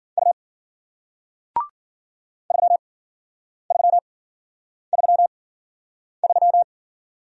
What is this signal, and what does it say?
Morse code, heard as a keyed tone, about 53 wpm, spaced as for 9 wpm, 700 Hz. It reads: U 3333